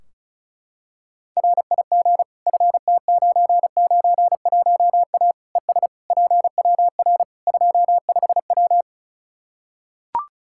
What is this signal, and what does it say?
Morse code, heard as a keyed tone, about 34 wpm, 700 Hz. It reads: RIG FT991A ES PWR 25W
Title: Practice QSO - Exchange 2 - Info - 35wpm